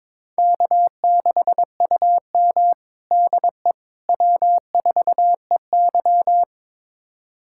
Morse code, heard as a keyed tone, about 22 wpm, 700 Hz. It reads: K6UM DE W4EY
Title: USA Callsign DE Callsign - 4 Characters - 22wpm